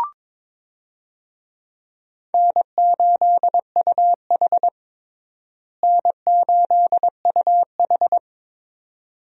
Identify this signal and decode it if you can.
Morse code, heard as a keyed tone, about 22 wpm, 700 Hz. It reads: N8UH N8UH
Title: USA Call Signs - 4 Characters - Repeated 2x - 22wpm